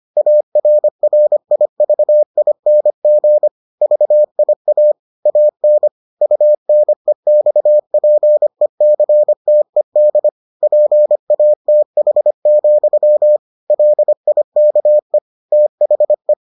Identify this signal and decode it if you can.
Morse code, heard as a keyed tone, about 25 wpm, 600 Hz. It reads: ARRIVING VIA AN UNEXPECTED PATH, LIKE THE